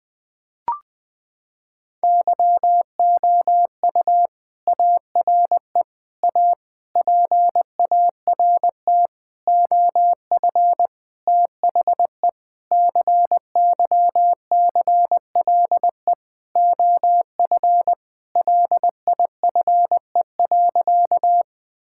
Morse code, heard as a keyed tone, about 20 wpm, 700 Hz. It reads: YOU ARE A PART OF THE CYCLE OF LIFE.
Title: Sentences from Top 1000 Words - Review Part 6 - 20wpm